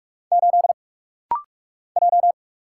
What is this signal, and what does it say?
Morse code, heard as a keyed tone, about 44 wpm, 700 Hz. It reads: 8 J
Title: Single Letter-Number - Rapid-Fire 45wpm